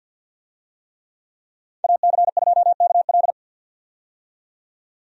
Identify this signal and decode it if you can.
Morse code, heard as a keyed tone, about 51 wpm, 700 Hz. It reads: AC2KL
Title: USA Call Signs - 5 Characters - 50wpm